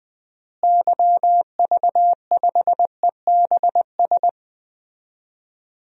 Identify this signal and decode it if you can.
Morse code, heard as a keyed tone, about 20 wpm, 700 Hz. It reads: YV5EBS
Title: DXCC-100 Call Signs - 6 Characters - Repeated 2x - 20wpm